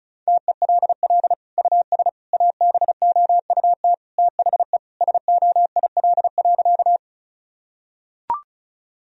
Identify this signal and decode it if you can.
Morse code, heard as a keyed tone, about 34 wpm, 700 Hz. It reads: TELL US ABOUT THE SOIL.
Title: Sentences from Top 1000 Words - Review Part 4 - 35wpm